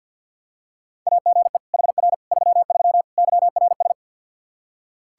Morse code, heard as a keyed tone, about 47 wpm, 700 Hz. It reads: AGE HR 33 YRS
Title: K1USN SST QSO Practice 50wpm